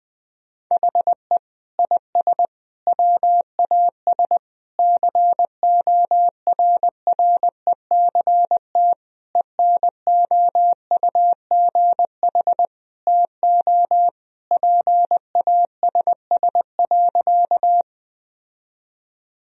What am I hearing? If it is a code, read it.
Morse code, heard as a keyed tone, about 20 wpm, 700 Hz. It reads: HE IS WAS CORRECT ENOUGH TO PASS.
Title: Sentences from Top 1000 Words - Review Part 2 - 20wpm